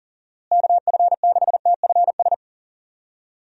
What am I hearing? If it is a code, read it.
Morse code, heard as a keyed tone, about 40 wpm, 700 Hz. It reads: KF6TFS
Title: USA Call Signs - 6 Characters - Repeated 2x - 40wpm